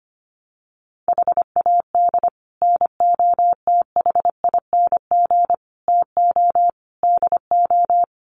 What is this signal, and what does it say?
Morse code, heard as a keyed tone, about 25 wpm, 700 Hz. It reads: HAD NOTHING TO DO
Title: Sets of 5 Words 25wpm